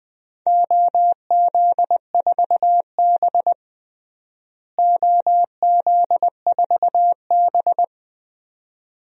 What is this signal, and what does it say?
Morse code, heard as a keyed tone, about 20 wpm, 700 Hz. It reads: OZ4B OZ4B
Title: DXCC-100 Call Signs - 4 Characters - Repeated 2x - 20wpm